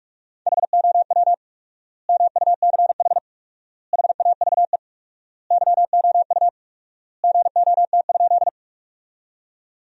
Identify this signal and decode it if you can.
Morse code, heard as a keyed tone, about 46 wpm, 700 Hz. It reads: HOW MUCH HAVE YOU GOT?
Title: 4x Word-Spacing - Sentences from Top 300 Words - 45wpm